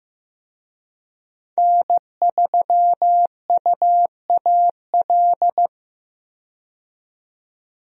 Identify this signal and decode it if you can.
Morse code, heard as a keyed tone, about 15 wpm, 700 Hz. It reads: N3UAL
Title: USA Call Signs - 5 Characters - 15wpm